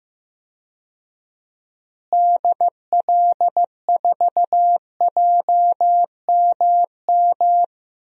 Morse code, heard as a keyed tone, about 15 wpm, 700 Hz. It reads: DL4JMM
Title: DXCC-100 Call Signs - 6 Characters - Repeated 2x - 15wpm